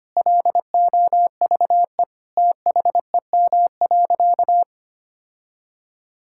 Morse code, heard as a keyed tone, about 25 wpm, 700 Hz. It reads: LOVE THEM.
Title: Sentences from Top 1600 Words 25wpm